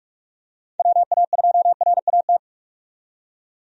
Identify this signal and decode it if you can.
Morse code, heard as a keyed tone, about 45 wpm, 700 Hz. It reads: WA2RAT
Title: USA Call Signs - 6 Characters - 45wpm